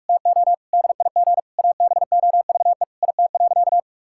Morse code, heard as a keyed tone, about 43 wpm, 700 Hz. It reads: TO DIG ABOVE IT.
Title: Sentences from Top 1600 Words 45wpm